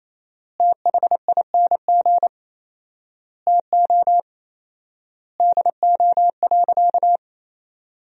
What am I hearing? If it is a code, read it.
Morse code, heard as a keyed tone, about 28 wpm, 700 Hz. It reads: THING TO DO.
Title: 4x Word-Spacing - Sentences from Top 100 Words - 28wpm